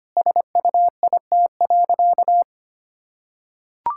Morse code, heard as a keyed tone, about 25 wpm, 700 Hz. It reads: SUIT.
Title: Sentences from Top 1600 Words 25wpm